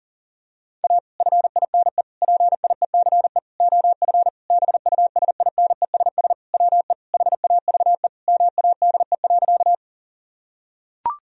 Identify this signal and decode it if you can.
Morse code, heard as a keyed tone, about 37 wpm, 700 Hz. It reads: A FINE PIECE OF BUSINESS WE HAVE MADE.
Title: Sentences from Top 1000 Words - Review Part 2 - 40wpm